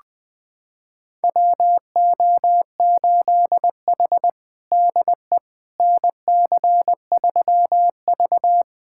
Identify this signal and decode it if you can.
Morse code, heard as a keyed tone, about 20 wpm, 700 Hz. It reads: WO8H DE NC3V
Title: USA Callsign DE Callsign - 4 Characters - 20wpm